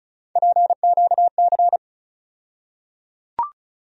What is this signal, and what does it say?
Morse code, heard as a keyed tone, about 36 wpm, 700 Hz. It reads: PQC
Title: USA Call Sign Postfixes - 3 Letters 35wpm